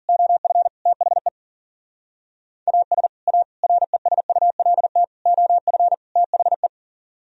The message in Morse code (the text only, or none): OF THE AS A RESULT OF THE